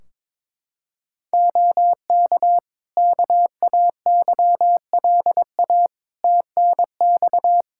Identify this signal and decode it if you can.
Morse code, heard as a keyed tone, about 22 wpm, 700 Hz. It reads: OK KAYLA TNX
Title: K1USN SST QSO Practice 22wpm